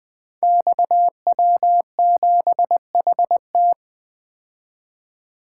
Morse code, heard as a keyed tone, about 21 wpm, 700 Hz. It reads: XW7HT
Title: Code Groups - 5 Letter:Numbers - 20wpm